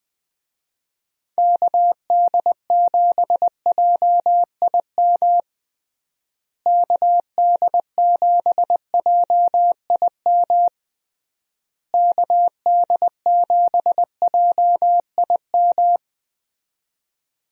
Morse code, heard as a keyed tone, about 21 wpm, 700 Hz. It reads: KD7JIM KD7JIM KD7JIM
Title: USA Call Signs - 6 Characters - Repeated 3x - 20wpm